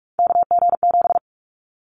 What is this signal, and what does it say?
Morse code, heard as a keyed tone, about 46 wpm, 700 Hz. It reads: KG7
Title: Callsign Prefixes for Top 25 DXCC - 45wpm